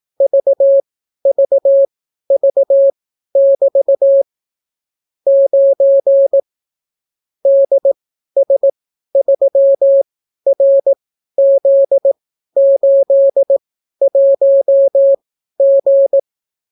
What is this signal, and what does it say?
Morse code, heard as a keyed tone, about 18 wpm, spaced as for 12 wpm, 550 Hz. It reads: VVV<BT> 9 DS3RZ81G